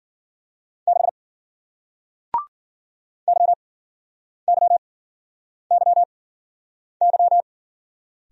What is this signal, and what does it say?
Morse code, heard as a keyed tone, about 49 wpm, spaced as for 9 wpm, 700 Hz. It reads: 6 YYYY
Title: Speed-Racing Single Letter-Number 40wpm